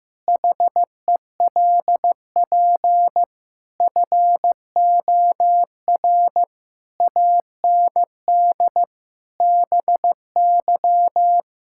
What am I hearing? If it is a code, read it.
Morse code, heard as a keyed tone, about 15 wpm, 700 Hz. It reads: HELP FOR AND BY
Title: Pangram Challenge v2 - 15wpm